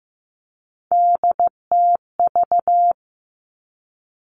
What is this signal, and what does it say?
Morse code, heard as a keyed tone, about 15 wpm, 700 Hz. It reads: DTV
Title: Three Challenging Letter-Number Combinations 15wpm